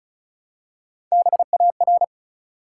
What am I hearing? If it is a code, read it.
Morse code, heard as a keyed tone, about 36 wpm, 700 Hz. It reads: BAR
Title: Three Letter Words - 35wpm